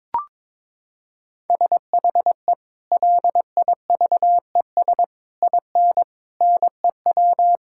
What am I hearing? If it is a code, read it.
Morse code, heard as a keyed tone, about 22 wpm, 700 Hz. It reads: SHE LIVES IN NEW
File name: practice / 22wpm / Sentences with 3-Word Phrases - 22wpm